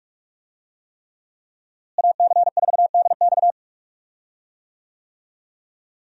Morse code, heard as a keyed tone, about 46 wpm, 700 Hz. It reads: AC4DX
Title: USA Call Signs - 5 Characters - Repeated 3x - 45wpm